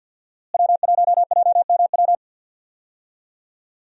Morse code, heard as a keyed tone, about 52 wpm, 700 Hz. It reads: W1JMW
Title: USA Call Signs - 5 Characters - Repeated 3x - 50wpm